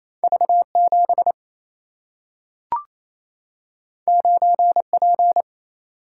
Morse code, heard as a keyed tone, about 28 wpm, 700 Hz. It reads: V 7 9 P
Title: Single-Letter Single-Number 28wpm